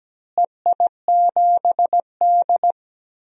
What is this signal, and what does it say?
Morse code, heard as a keyed tone, about 17 wpm, 700 Hz. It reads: EI7D
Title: DXCC-100 Call Signs - 4 Characters - Repeated 2x - 17wpm